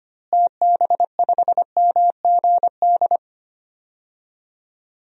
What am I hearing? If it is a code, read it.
Morse code, heard as a keyed tone, about 25 wpm, 700 Hz. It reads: TB5MGD
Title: DXCC-100 Call Signs - 6 Characters - Repeated 2x - 25wpm